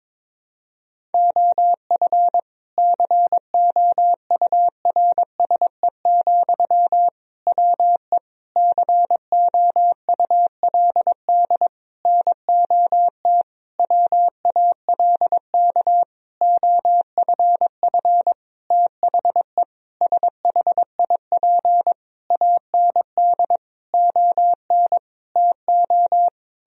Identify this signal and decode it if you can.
Morse code, heard as a keyed tone, about 22 wpm, 700 Hz. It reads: OF COURSE, WE COULD NOT WALK OFF THE SHIP AND ON TO
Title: Sentences from Top 1000 Words - Review Part 2 - 22wpm